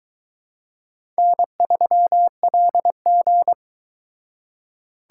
Morse code, heard as a keyed tone, about 22 wpm, 700 Hz. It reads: N3LG